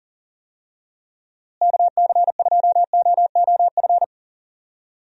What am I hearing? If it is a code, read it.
Morse code, heard as a keyed tone, about 39 wpm, 700 Hz. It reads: KC2OOF
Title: USA Call Signs - 6 Characters - Repeated 3x - 40wpm